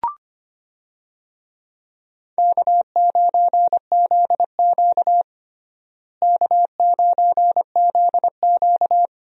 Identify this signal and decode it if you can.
Morse code, heard as a keyed tone, about 25 wpm, 700 Hz. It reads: K9ZQ K9ZQ
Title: USA Call Signs - 4 Characters - Repeated 2x - 25wpm